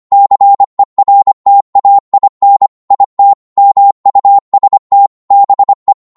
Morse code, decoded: CERTAIN IT MUST BE